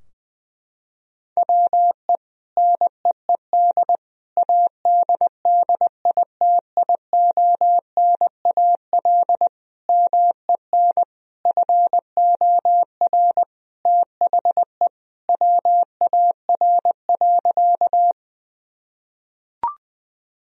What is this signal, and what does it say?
Morse code, heard as a keyed tone, about 20 wpm, 700 Hz. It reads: WE NEED ADDITIONAL MEN FOR THE WAR.
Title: Sentences from Top 1000 Words - Review Part 4 - 20wpm